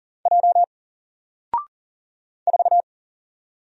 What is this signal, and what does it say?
Morse code, heard as a keyed tone, about 40 wpm, 700 Hz. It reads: J 4
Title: Single Letter-Number 40wpm